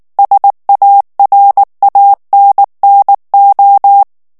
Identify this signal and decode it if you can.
Morse code, heard as a keyed tone, about 19 wpm, 800 Hz. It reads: SARANNO
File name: morse mp3 / ITALIEN texte clair 2 vitesse 18 mots minute